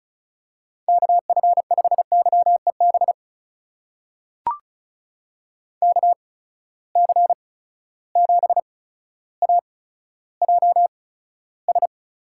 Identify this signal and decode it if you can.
Morse code, heard as a keyed tone, about 34 wpm, spaced as for 15 wpm, 700 Hz. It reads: KF5YEB KC7AJS
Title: USA Call Signs - 6 Characters - 8x Character Spacing - 35wpm